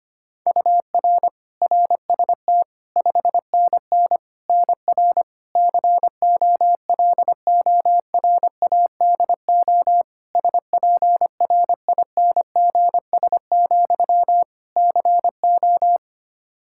Morse code, heard as a keyed tone, about 25 wpm, 700 Hz. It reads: UR RST 5NN NR COLORADO SPRINGS, CO
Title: Practice QSO - Exchange 1 - Beginning - 25wpm